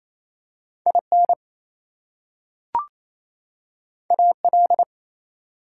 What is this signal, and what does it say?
Morse code, heard as a keyed tone, about 28 wpm, 700 Hz. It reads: IN AL
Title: US States and Canadian Provinces 28wpm